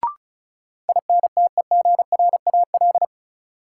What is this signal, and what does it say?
Morse code, heard as a keyed tone, about 36 wpm, 700 Hz. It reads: INTEGRAL
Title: Sets of 5 Words - New Word Drill - 35wpm